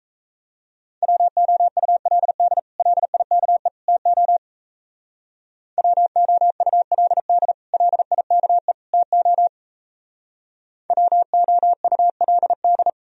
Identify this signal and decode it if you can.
Morse code, heard as a keyed tone, about 34 wpm, 700 Hz. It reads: WOULD LIKE TO WOULD LIKE TO WOULD
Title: Speed-Racing - Sets of 3 Words 28wpm